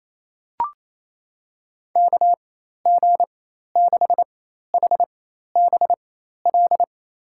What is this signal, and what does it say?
Morse code, heard as a keyed tone, about 28 wpm, spaced as for 13 wpm, 700 Hz. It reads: KG6HBL
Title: USA Call Signs - 4x Character Spacing - 28wpm